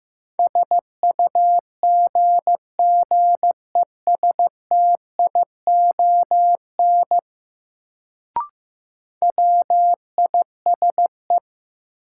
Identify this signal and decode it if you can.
Morse code, heard as a keyed tone, about 15 wpm, 700 Hz. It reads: SUGGESTION WISE
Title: Sentences from Top 1600 Words - New Word Drill - 15wpm